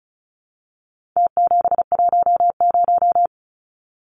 Morse code, heard as a keyed tone, about 34 wpm, 700 Hz. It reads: T710
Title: Numbers - Contest Style - 4-Digit 35wpm